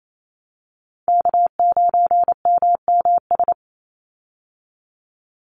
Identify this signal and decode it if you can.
Morse code, heard as a keyed tone, about 28 wpm, 700 Hz. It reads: K9MMS
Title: Call Signs Repeated 2x - 28wpm